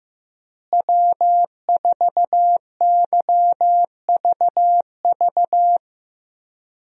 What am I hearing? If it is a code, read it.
Morse code, heard as a keyed tone, about 15 wpm, 700 Hz. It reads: W4YVV
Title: USA Call Signs - 5 Characters - Repeated 2x - 15wpm